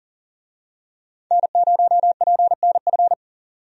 Morse code, heard as a keyed tone, about 40 wpm, 700 Hz. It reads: N0PNF
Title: USA Call Signs - 5 Characters - 40wpm